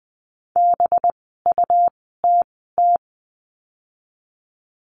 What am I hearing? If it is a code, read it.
Morse code, heard as a keyed tone, about 20 wpm, spaced as for 15 wpm, 700 Hz. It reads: BUTT
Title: 185 Lesson 30f - Letter b - Words 20wpm